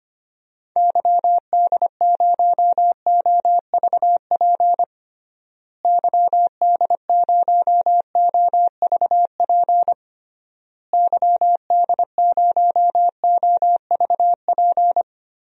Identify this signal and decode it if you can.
Morse code, heard as a keyed tone, about 25 wpm, 700 Hz. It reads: YD0OVP YD0OVP YD0OVP
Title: DXCC-100 Call Signs - 6 Characters - Repeated 3x - 25wpm